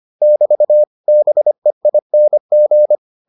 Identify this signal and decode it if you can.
Morse code, heard as a keyed tone, about 25 wpm, 600 Hz. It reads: <BT> BEING